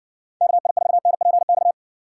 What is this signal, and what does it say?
Morse code, heard as a keyed tone, about 55 wpm, 700 Hz. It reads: KI4NPX